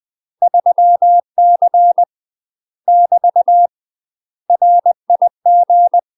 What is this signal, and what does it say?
Morse code, heard as a keyed tone, about 20 wpm, 700 Hz. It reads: 3C <BT> RIG
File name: UK MASTER QSOs TRAINING / 20WPM / UK MASTER QSOs 0063